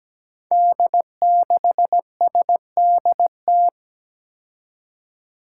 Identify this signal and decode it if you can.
Morse code, heard as a keyed tone, about 17 wpm, 700 Hz. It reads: D6SDT
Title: Five Challenging Letter-Number Combinations 17wpm